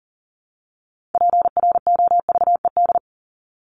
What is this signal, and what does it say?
Morse code, heard as a keyed tone, about 40 wpm, 700 Hz. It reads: PROVED
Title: ED Suffix - Top 100 Words with Base Words 40wpm